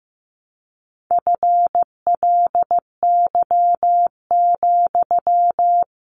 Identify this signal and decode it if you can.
Morse code, heard as a keyed tone, about 15 wpm, 700 Hz. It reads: FLY,
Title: Sentences from Top 500 Words 15wpm